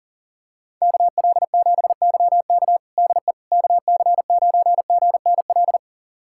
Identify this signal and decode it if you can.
Morse code, heard as a keyed tone, about 40 wpm, 700 Hz. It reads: KL7YK DE KC9GNL
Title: USA Callsign DE Callsign 40wpm